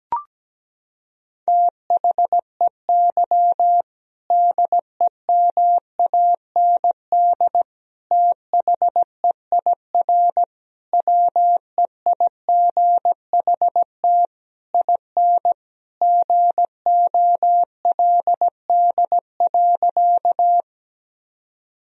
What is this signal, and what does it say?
Morse code, heard as a keyed tone, about 17 wpm, 700 Hz. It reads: THEY DEMAND THEIR WEIGHT IN GOLD.
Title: Sentences from Top 1000 Words - Review Part 3 - 17wpm